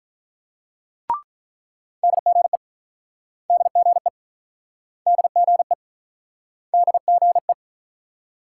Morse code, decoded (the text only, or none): DGE DGE DGE DGE